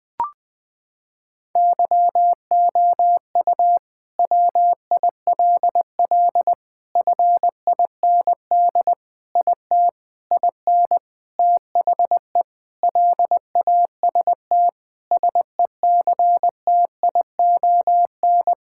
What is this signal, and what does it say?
Morse code, heard as a keyed tone, about 20 wpm, 700 Hz. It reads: YOU WILL FIND IT IN THE LAST SECTION
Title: Sentences from Top 1000 Words - Review Part 5 - 20wpm